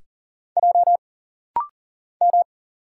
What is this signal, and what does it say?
Morse code, heard as a keyed tone, about 40 wpm, 700 Hz. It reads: J M
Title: Single Letter-Number Alt - Rapid-Fire 40wpm